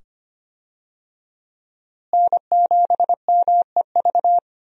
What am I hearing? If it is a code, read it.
Morse code, heard as a keyed tone, about 25 wpm, 700 Hz. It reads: N7MEV
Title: USA Call Signs - 5 Characters - Repeated 2x - 25wpm